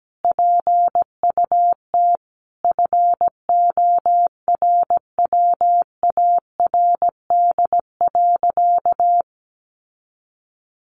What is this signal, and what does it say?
Morse code, heard as a keyed tone, about 17 wpm, 700 Hz. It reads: PUT FORWARD.
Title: Sentences from Top 1300 Words 17wpm